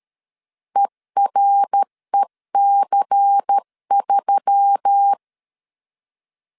Morse code, heard as a keyed tone, about 13 wpm, 800 Hz. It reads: EREC3
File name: CW / DET / 1 6 1500C S C 12wpm 3533